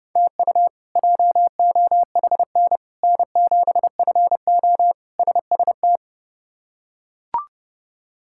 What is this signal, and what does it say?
Morse code, heard as a keyed tone, about 31 wpm, 700 Hz. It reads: TU JOHN N7FO SST
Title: K1USN SST QSO Practice - 30wpm